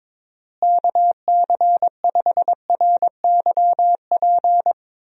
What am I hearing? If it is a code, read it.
Morse code, heard as a keyed tone, about 22 wpm, 700 Hz. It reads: KC5RYP
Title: USA Call Signs - 6 Characters - Repeated 2x - 22wpm